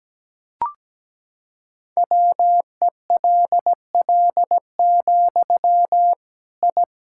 Morse code, heard as a keyed tone, about 17 wpm, 700 Hz. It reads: WELL, I
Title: Sentences from Top 1000 Words - Review Part 5 - 17wpm